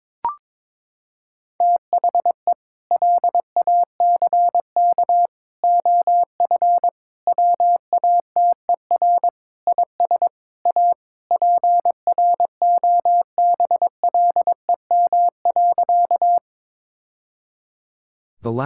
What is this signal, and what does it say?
Morse code, heard as a keyed tone, about 22 wpm, 700 Hz. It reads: THE LACK OF WATER IS A PROBLEM.
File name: practice / 22wpm / Sentences with 3-Word Phrases - 22wpm